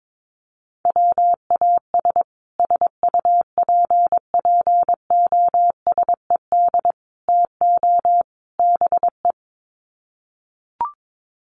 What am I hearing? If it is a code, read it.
Morse code, heard as a keyed tone, about 22 wpm, 700 Hz. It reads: WAS SUPPOSED TO BE
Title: Sets of 4 Words 22wpm